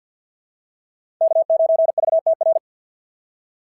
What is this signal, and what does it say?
Morse code, heard as a keyed tone, about 52 wpm, 690 Hz. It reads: K9VTR
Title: SOTA QSO Practice 50wpm